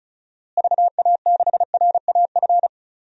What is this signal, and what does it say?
Morse code, heard as a keyed tone, about 36 wpm, 700 Hz. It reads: VA6RAF
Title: DXCC-100 Call Signs - 6 Characters - Repeated 2x - 35wpm